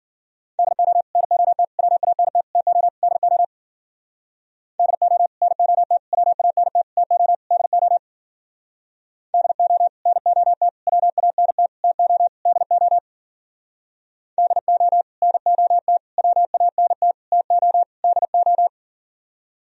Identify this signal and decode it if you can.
Morse code, heard as a keyed tone, about 48 wpm, 700 Hz. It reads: DO NOT WANT TO DO DO NOT WANT TO DO DO NOT WANT TO DO DO NOT WANT TO DO
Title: Speed-Racing - Sets of 5 Words 40wpm